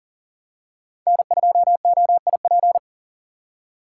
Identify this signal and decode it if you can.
Morse code, heard as a keyed tone, about 38 wpm, 700 Hz. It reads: N2OIP
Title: USA Call Signs - 5 Characters - 2x Character Spacing - 40wpm